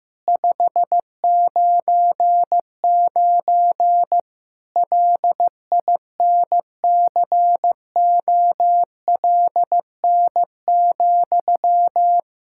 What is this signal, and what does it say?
Morse code, heard as a keyed tone, about 16 wpm, 700 Hz. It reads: 599 LINCOLN,
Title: Practice QSO - Exchange 1 - Beginning - 15wpm